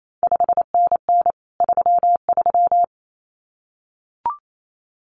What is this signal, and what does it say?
Morse code, heard as a keyed tone, about 28 wpm, 700 Hz. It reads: 5NN 33
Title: CQ WW DX Contest Exchange Encore - 28wpm